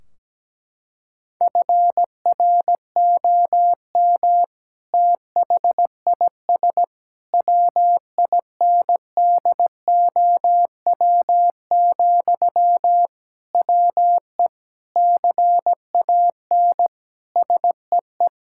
Morse code, heard as a keyed tone, about 17 wpm, 700 Hz. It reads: FROM THIS WINDOW, WE CAN SEE
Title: Sentences from Top 1600 Words 17wpm